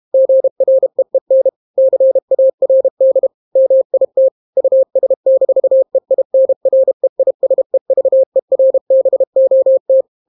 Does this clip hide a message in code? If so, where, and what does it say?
Yes: Morse code from 0.1 to 10.0 s, heard as a keyed tone, about 32 wpm, 530 Hz. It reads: GREEN CARD MIT US-EINREISEVERBOT